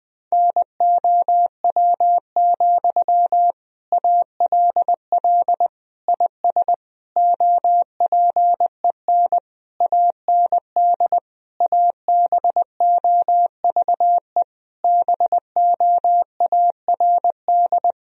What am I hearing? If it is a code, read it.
Morse code, heard as a keyed tone, about 20 wpm, 700 Hz. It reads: NOW, ALL IS OPEN AND ABOVE BOARD
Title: Sentences from Top 1000 Words - Review Part 4 - 20wpm